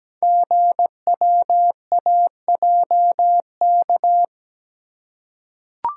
Morse code, heard as a keyed tone, about 17 wpm, 690 Hz. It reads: GWAJK
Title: Code Groups - 5 Letter:Numbers - 17wpm